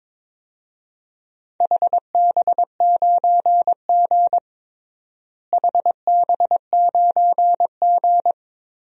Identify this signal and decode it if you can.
Morse code, heard as a keyed tone, about 21 wpm, 700 Hz. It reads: HB9G HB9G
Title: DXCC-100 Call Signs - 4 Characters - Repeated 2x - 22wpm